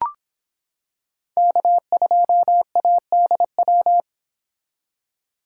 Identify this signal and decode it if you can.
Morse code, heard as a keyed tone, about 26 wpm, 700 Hz. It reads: K2ADW